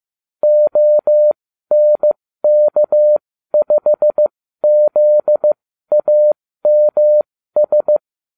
Morse code, heard as a keyed tone, about 15 wpm, spaced as for 12 wpm, 600 Hz. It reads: ONK5ZAMS